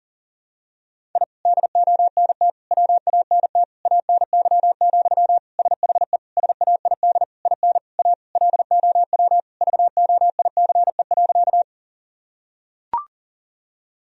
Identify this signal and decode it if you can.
Morse code, heard as a keyed tone, about 40 wpm, 700 Hz. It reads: I DONT WANT ANY, SHE SAID IN A LOW VOICE.
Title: Sentences from Top 1000 Words - Review Part 1 - 40wpm